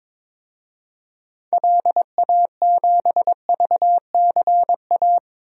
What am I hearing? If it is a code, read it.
Morse code, heard as a keyed tone, about 22 wpm, 700 Hz. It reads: LA7VCA
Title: DXCC-100 Call Signs - 6 Characters - 22wpm